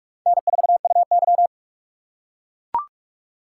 Morse code, heard as a keyed tone, about 40 wpm, 700 Hz. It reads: N4UY